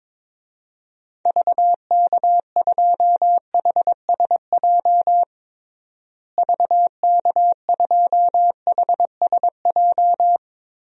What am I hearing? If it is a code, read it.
Morse code, heard as a keyed tone, about 22 wpm, 700 Hz. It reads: VK2HSJ VK2HSJ
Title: DXCC-100 Call Signs - 6 Characters - Repeated 3x - 22wpm